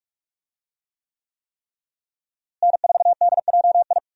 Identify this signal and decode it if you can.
Morse code, heard as a keyed tone, about 44 wpm, 700 Hz. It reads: N4DJI
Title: USA Call Signs - 5 Characters - Repeated 3x - 45wpm